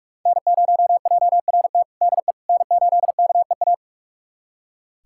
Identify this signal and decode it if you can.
Morse code, heard as a keyed tone, about 46 wpm, 700 Hz. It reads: N0JRT DE N8KEA
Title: USA Callsign DE Callsign - 5 Characters - 45wpm